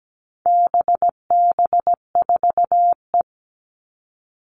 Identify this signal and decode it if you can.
Morse code, heard as a keyed tone, about 17 wpm, 700 Hz. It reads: BB4E
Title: All Cut Numbers - 4-Digit 17wpm